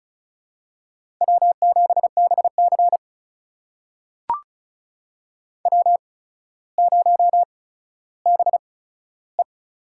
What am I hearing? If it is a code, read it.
Morse code, heard as a keyed tone, about 36 wpm, spaced as for 15 wpm, 700 Hz. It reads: W7BC W0BE
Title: USA Call Signs - 4 Characters - 8x Character Spacing - 35wpm